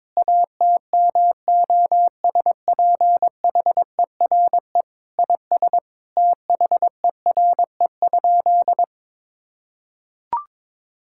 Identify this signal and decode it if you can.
Morse code, heard as a keyed tone, about 21 wpm, 700 Hz. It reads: ATMOSPHERE IS THERE?
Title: Sentences from Top 1600 Words 22wpm